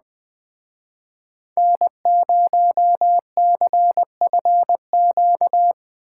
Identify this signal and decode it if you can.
Morse code, heard as a keyed tone, about 20 wpm, 700 Hz. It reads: N0CFQ
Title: Speed Racing - USA 1x3 Call Signs 17wpm